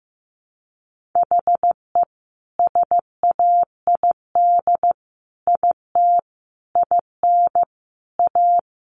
Morse code, heard as a keyed tone, about 15 wpm, 700 Hz. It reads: HE SAID IT IN A
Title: Sentences from Top 1500 Words 15wpm